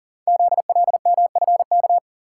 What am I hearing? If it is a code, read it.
Morse code, heard as a keyed tone, about 38 wpm, 690 Hz. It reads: ZLMFK
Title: Code Groups - 5 Letters - 40wpm